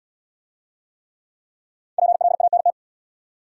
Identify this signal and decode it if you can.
Morse code, heard as a keyed tone, about 67 wpm, 700 Hz. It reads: PLANT